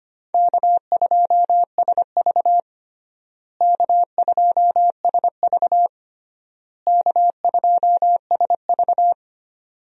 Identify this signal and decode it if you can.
Morse code, heard as a keyed tone, about 25 wpm, 700 Hz. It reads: K2SV K2SV K2SV
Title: USA Call Signs - 4 Characters - Repeated 3x - 25wpm